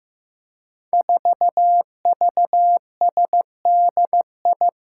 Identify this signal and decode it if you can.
Morse code, heard as a keyed tone, about 15 wpm, 700 Hz. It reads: 4VSDI
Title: Five Challenging Letter-Number Combinations 15wpm